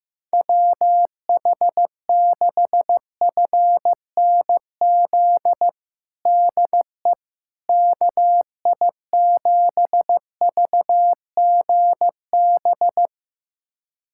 Morse code, heard as a keyed tone, about 15 wpm, 700 Hz. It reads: WH6FNZ DE KI7VGB
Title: USA Callsign DE Callsign - 6 Characters - 15wpm